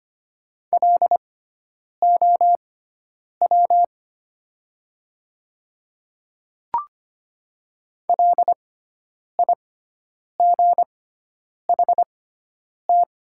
Morse code, heard as a keyed tone, about 25 wpm, spaced as for 9 wpm, 700 Hz. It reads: LOW LIGHT
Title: 6x Character Spacing - Top 200 Words - 25wpm